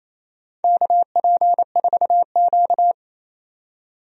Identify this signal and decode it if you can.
Morse code, heard as a keyed tone, about 28 wpm, 700 Hz. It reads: KP4Q
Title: DXCC-100 Call Signs - 4 Characters - 28wpm